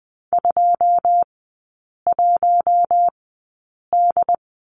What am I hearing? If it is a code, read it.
Morse code, heard as a keyed tone, about 20 wpm, spaced as for 8 wpm, 700 Hz. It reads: 21D
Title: 087 Lesson 15e - Number 5 - Three Characters 20wpm